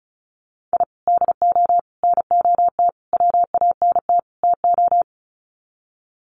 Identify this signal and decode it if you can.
Morse code, heard as a keyed tone, about 34 wpm, 700 Hz. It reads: I DO NOT WANT TO
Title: Sets of 5 Words 35wpm